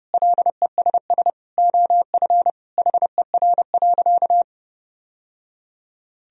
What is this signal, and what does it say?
Morse code, heard as a keyed tone, about 30 wpm, 700 Hz. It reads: LESS OF HER.
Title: Sentences from Top 1000 Words - Review Part 2 - 30wpm